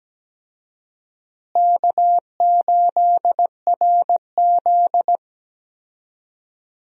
Morse code, heard as a keyed tone, about 17 wpm, 700 Hz. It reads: K8RZ
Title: USA Call Signs - 4 Characters - Repeated 3x - 17wpm